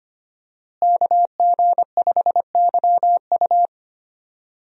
Morse code, heard as a keyed tone, about 25 wpm, 700 Hz. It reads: KG5YU
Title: USA Call Signs - 5 Characters - Repeated 2x - 25wpm